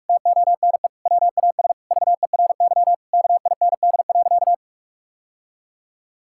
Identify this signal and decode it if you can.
Morse code, heard as a keyed tone, about 46 wpm, 700 Hz. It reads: TONE WAS VERY KIND.
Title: Sentences from Top 1600 Words 45wpm